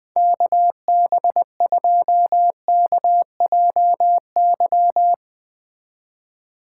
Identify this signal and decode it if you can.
Morse code, heard as a keyed tone, about 19 wpm, 700 Hz. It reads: KB2KJY
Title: Speed Racing - USA 2x3 Call Signs 17wpm